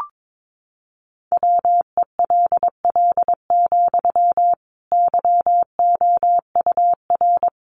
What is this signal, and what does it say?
Morse code, heard as a keyed tone, about 22 wpm, 700 Hz. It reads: WELL, YOUR
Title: Sentences from Top 1400 Words 22wpm